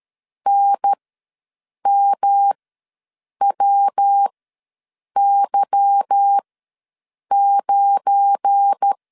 Morse code, heard as a keyed tone, about 13 wpm, spaced as for 7 wpm, 800 Hz. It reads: NMWY9
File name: LEZIONI / 9 2 ws 12wpm 3406